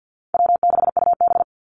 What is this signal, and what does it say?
Morse code, heard as a keyed tone, about 50 wpm, 700 Hz. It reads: W6UB
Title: NA SOTA Activator Callsigns 50wpm